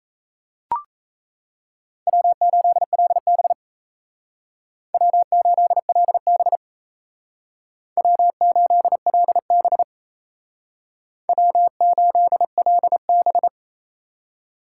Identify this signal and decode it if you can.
Morse code, heard as a keyed tone, about 36 wpm, 700 Hz. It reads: W8LB W8LB W8LB W8LB